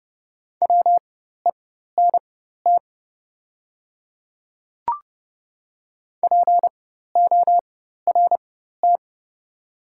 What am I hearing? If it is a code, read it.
Morse code, heard as a keyed tone, about 30 wpm, spaced as for 14 wpm, 700 Hz. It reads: WENT PORT
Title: 4x Character Spacing - Top 200 Words - 30wpm